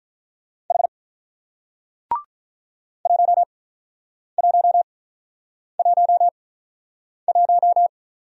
Morse code, heard as a keyed tone, about 47 wpm, spaced as for 9 wpm, 700 Hz. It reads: H 1111